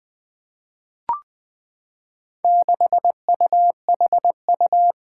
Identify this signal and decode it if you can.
Morse code, heard as a keyed tone, about 21 wpm, 700 Hz. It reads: 6UHU
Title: Four Challenging Letter-Number Combinations 20wpm